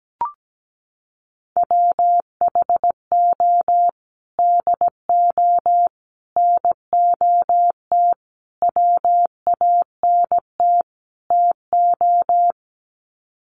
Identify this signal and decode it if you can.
Morse code, heard as a keyed tone, about 17 wpm, 700 Hz. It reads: WHO DO NOT WANT TO
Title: Sets of 5 Words Encore2 17wpm